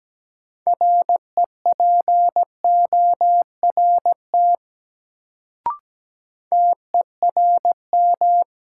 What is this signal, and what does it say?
Morse code, heard as a keyed tone, about 17 wpm, 700 Hz. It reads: REPORT TERM
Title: Sentences from Top 900 Words - New Word Drill - 17wpm